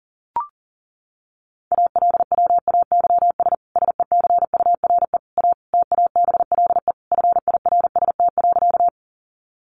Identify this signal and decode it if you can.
Morse code, heard as a keyed tone, about 40 wpm, 700 Hz. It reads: ALWAYS SECURE A TABLE FIRST.
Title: Sentences from Top 1200 Words 40wpm